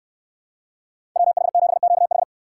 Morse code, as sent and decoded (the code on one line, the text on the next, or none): .-- .... --... --.- ....
WH7QH